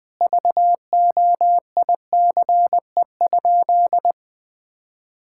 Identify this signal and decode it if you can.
Morse code, heard as a keyed tone, about 20 wpm, 700 Hz. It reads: VOICE?
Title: Sentences from Top 1600 Words 20wpm